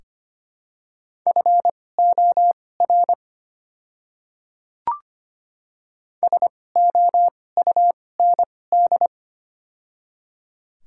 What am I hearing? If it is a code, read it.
Morse code, heard as a keyed tone, about 25 wpm, spaced as for 18 wpm, 700 Hz. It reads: FOR SOUND
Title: 2x Character Spacing - Top 100 Words - 25wpm